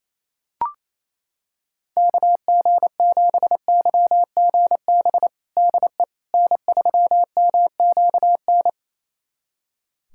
Morse code, heard as a keyed tone, about 28 wpm, 700 Hz. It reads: KG7YGB DE N3MQN
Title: USA Callsign DE Callsign 28wpm